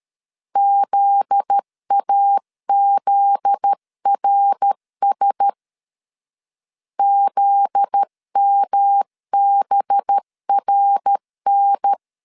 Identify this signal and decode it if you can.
Morse code, heard as a keyed tone, about 13 wpm, 800 Hz. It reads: ZAZRS ZMBRN
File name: LEZIONI / 7 GF9Z / 7 GF9Z 12wpm 2147